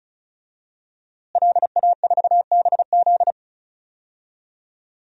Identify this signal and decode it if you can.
Morse code, heard as a keyed tone, about 35 wpm, 700 Hz. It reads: LA4BZ
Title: DXCC-100 Call Signs - 5 Characters - Repeated 2x - 35wpm